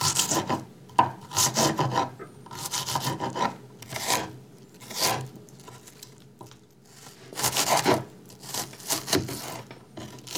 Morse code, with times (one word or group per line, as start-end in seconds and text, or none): none